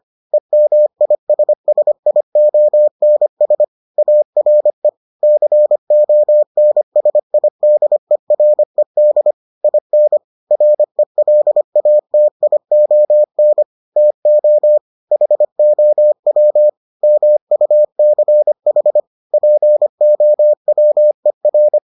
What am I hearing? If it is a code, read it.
Morse code, heard as a keyed tone, about 25 wpm, 600 Hz. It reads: EMISSIONS ARE CONSIDERED IN RELATION TO HOW MUCH POWER